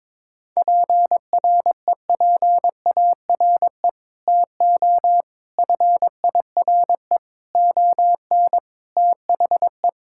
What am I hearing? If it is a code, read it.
Morse code, heard as a keyed tone, about 21 wpm, 700 Hz. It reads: PREPARE TO FIRE ON THE
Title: Sentences from Top 1000 Words - Review Part 6 - 22wpm